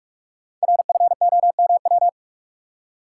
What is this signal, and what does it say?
Morse code, heard as a keyed tone, about 44 wpm, 700 Hz. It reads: RFOMW
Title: Code Groups - 5 Letters - 45wpm